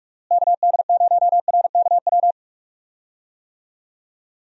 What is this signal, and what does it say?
Morse code, heard as a keyed tone, about 47 wpm, 700 Hz. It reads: KD0RKW
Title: USA Call Signs - 6 Characters - Repeated 3x - 45wpm